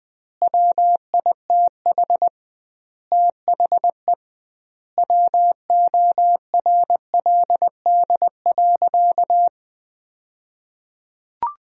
Morse code, heard as a keyed tone, about 21 wpm, 700 Hz. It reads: WITH THE WORLD.
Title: 2x Word-Spacing - Sentences from Top 200 Words - 20wpm